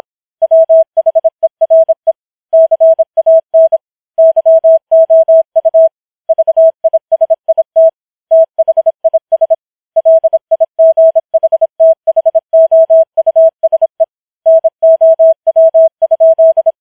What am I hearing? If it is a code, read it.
Morse code, heard as a keyed tone, about 26 wpm, 650 Hz. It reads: WHERE CAN YOU VISIT THIS LIGHTHOUSE NOW?
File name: challenge191